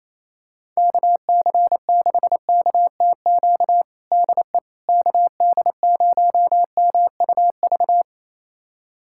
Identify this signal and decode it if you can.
Morse code, heard as a keyed tone, about 28 wpm, 700 Hz. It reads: KC6KTQ DE KD0MUV
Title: USA Callsign DE Callsign 28wpm